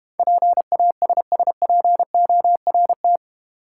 Morse code, heard as a keyed tone, about 31 wpm, 700 Hz. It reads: PASSPORT
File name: Easy Words - Speed-Racing with Hints - 25wpm